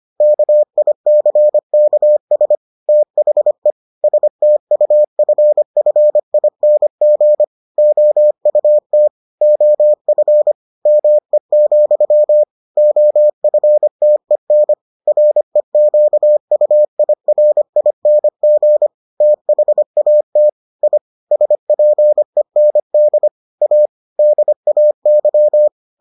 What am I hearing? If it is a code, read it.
Morse code, heard as a keyed tone, about 25 wpm, 600 Hz. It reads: KICKS THE STUFFING OUT OF ME, OFTEN REQUIRING THAT I SPEND A DAY